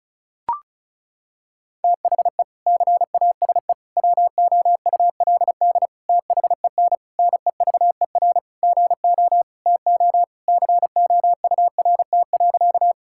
Morse code, heard as a keyed tone, about 34 wpm, 700 Hz. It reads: THE CASE WOULD THEN NEVER GO TO COURT.
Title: Sentences from Top 1600 Words 35wpm